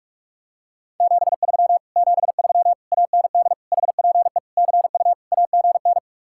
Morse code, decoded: 73 73 AND HPE CU AGN

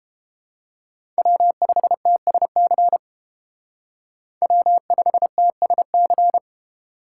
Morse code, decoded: W5TSC W5TSC

